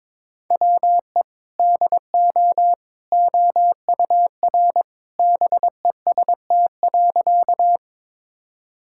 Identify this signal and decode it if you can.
Morse code, heard as a keyed tone, about 22 wpm, 700 Hz. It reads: WE DO OUR BEST.
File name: practice / 22wpm / Sentences with 2-Word Phrases - 22wpm